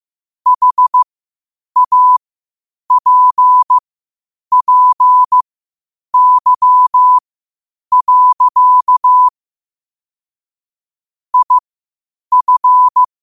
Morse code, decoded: HAPPY. IF

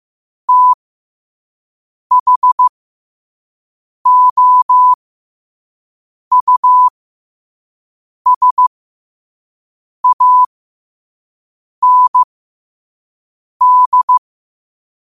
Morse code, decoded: THOUSAND

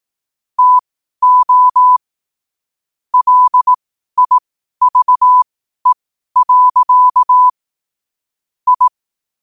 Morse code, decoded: TO LIVE. I

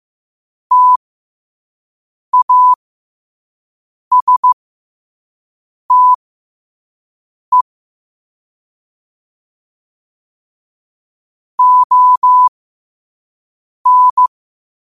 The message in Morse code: TASTE ON

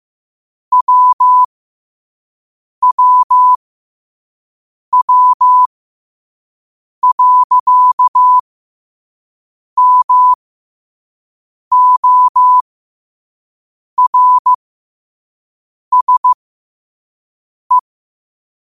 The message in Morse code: WWW.MORSE

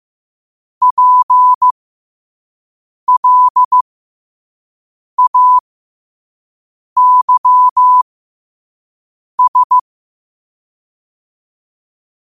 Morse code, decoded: PLAYS

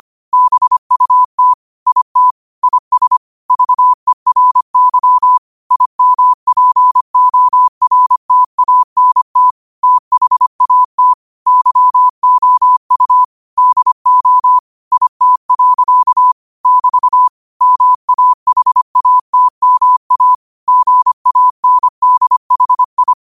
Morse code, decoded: BUT IT IS VERY IMPORTANT THAT YOU DO IT. <BT> MAHATMA GANDHI

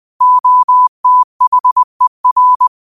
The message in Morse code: OTHER

